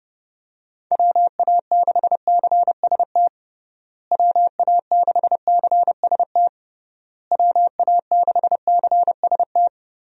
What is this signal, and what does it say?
Morse code, heard as a keyed tone, about 30 wpm, 700 Hz. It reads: WA6CST WA6CST WA6CST
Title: USA Call Signs - 6 Characters - Repeated 3x - 30wpm